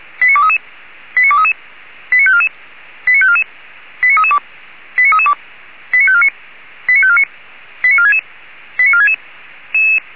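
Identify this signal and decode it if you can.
Five-tone selective call (ZVEI), heard as quick runs of five tones, tones of about 70 ms. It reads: 89233 89233 89544 89544 89332 89332 89549 89549 89466 89466
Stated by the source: ZVEI five-tone call received by SDR.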